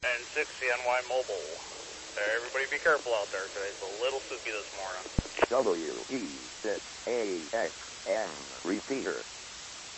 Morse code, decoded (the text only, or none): none